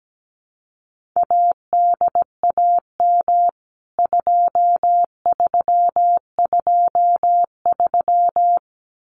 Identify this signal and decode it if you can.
Morse code, heard as a keyed tone, about 17 wpm, 700 Hz. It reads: ADAM 2323
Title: CWT Exchange 17wpm